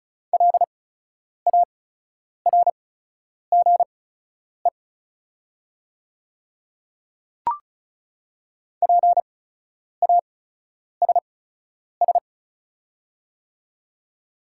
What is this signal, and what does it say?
Morse code, heard as a keyed tone, about 34 wpm, spaced as for 10 wpm, 690 Hz. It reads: LARGE PASS
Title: 8x Character Spacing - Top 500 Words - 35wpm